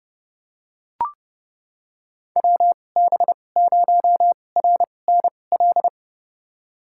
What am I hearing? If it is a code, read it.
Morse code, heard as a keyed tone, about 30 wpm, spaced as for 22 wpm, 700 Hz. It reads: WB0RNL
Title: USA Call Signs - 6 Characters - 2x Character Spacing - 30wpm